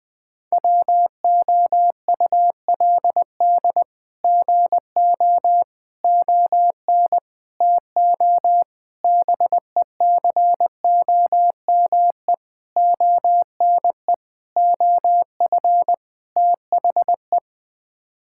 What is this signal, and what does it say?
Morse code, heard as a keyed tone, about 20 wpm, 700 Hz. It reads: WOULD GO ON TO BECOME ONE OF THE
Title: Sets of 8 Words - 20wpm